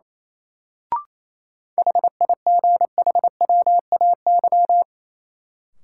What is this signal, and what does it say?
Morse code, heard as a keyed tone, about 28 wpm, 700 Hz. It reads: HIGHWAY
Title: Sentences from Top 1400 Words - New Word Drill - 28wpm